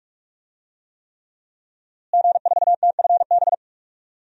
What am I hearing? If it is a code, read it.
Morse code, heard as a keyed tone, about 44 wpm, 700 Hz. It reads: G4TFB
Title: DXCC-100 Call Signs - 5 Characters - Repeated 3x - 45wpm